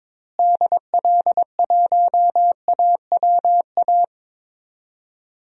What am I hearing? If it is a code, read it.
Morse code, heard as a keyed tone, about 21 wpm, 700 Hz. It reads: DL1AWA
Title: DXCC-100 Call Signs - 6 Characters - Repeated 3x - 22wpm